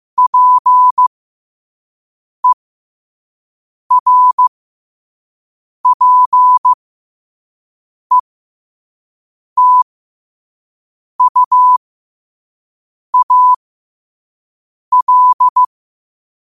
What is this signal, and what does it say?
Morse code, heard as a keyed tone, about 15 wpm, spaced as for 5 wpm, 1000 Hz. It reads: PERPETUAL
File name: qotdfiles / 5wpm